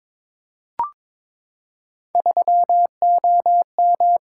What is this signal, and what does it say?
Morse code, heard as a keyed tone, about 22 wpm, 700 Hz. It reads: 3OM
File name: practice / 22wpm / Three Letter-Number Combinations - 22wpm